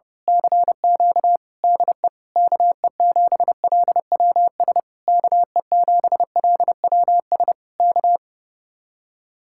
Morse code, heard as a keyed tone, about 28 wpm, 700 Hz. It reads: CQ DE KE7LWS KE7LWS K
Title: Speed-Racing Common QSO Phrases - 30wpm